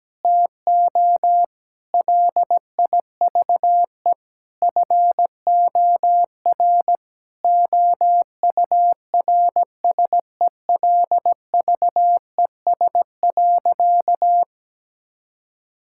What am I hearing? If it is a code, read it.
Morse code, heard as a keyed tone, about 17 wpm, 700 Hz. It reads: TO LIVE FOR OURSELVES.
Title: Sentences from Top 1600 Words 17wpm